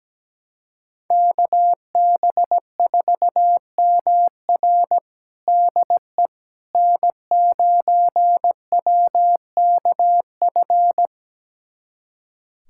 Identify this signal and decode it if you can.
Morse code, heard as a keyed tone, about 17 wpm, 700 Hz. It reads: KB4MR DE N9WKF
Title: USA Callsign DE Callsign - 5 Characters - 17wpm